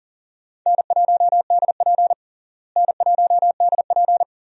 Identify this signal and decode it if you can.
Morse code, heard as a keyed tone, about 40 wpm, 700 Hz. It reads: N1DP N1DP
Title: USA Call Signs - 4 Characters - Repeated 2x - 40wpm